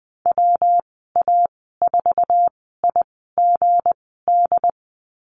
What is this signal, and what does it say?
Morse code, heard as a keyed tone, about 20 wpm, spaced as for 15 wpm, 700 Hz. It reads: WA4IGD
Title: 213 Lesson 34g - Number 8 - Call Signs 20wpm